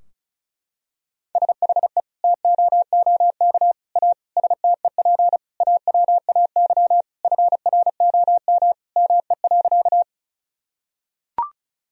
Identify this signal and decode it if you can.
Morse code, heard as a keyed tone, about 34 wpm, 700 Hz. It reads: SHE TOOK A STEP AWAY FROM ME.
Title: Sentences from Top 1000 Words - Review Part 2 - 35wpm